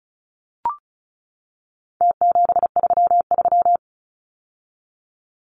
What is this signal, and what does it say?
Morse code, heard as a keyed tone, about 34 wpm, 700 Hz. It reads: T733
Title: Numbers - Contest Style - 4-Digit 35wpm